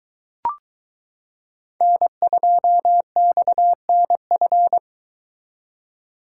Morse code, heard as a keyed tone, about 22 wpm, 700 Hz. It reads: N2XNF